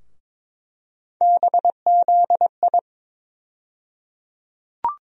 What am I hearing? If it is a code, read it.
Morse code, heard as a keyed tone, about 22 wpm, 700 Hz. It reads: BZI
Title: USA Call Sign Postfixes - 3 Letters 22wpm